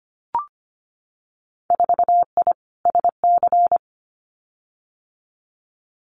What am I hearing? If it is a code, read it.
Morse code, heard as a keyed tone, about 25 wpm, 690 Hz. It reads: 4I SC
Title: Winter Field Day Exchanges 25wpm